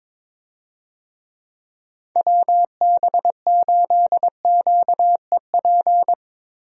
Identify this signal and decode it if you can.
Morse code, heard as a keyed tone, about 21 wpm, 700 Hz. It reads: WB8QEP
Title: USA Call Signs - 6 Characters - Repeated 2x - 22wpm